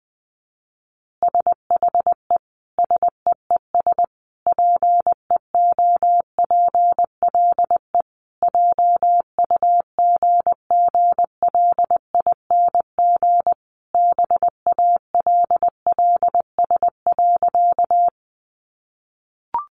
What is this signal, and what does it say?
Morse code, heard as a keyed tone, about 20 wpm, 700 Hz. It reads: SHE SEES PEOPLE JUGGLING BALLS.
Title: Easy Phrases 20wpm